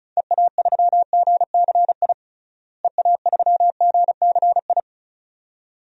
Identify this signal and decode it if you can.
Morse code, heard as a keyed tone, about 34 wpm, 700 Hz. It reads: EA3GCI EA3GCI
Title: DXCC-100 Call Signs - 6 Characters - Repeated 2x - 35wpm